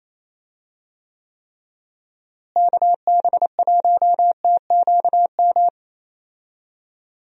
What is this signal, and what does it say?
Morse code, heard as a keyed tone, about 28 wpm, 700 Hz. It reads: KB1TQM
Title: USA Call Signs - 6 Characters - Repeated 3x - 28wpm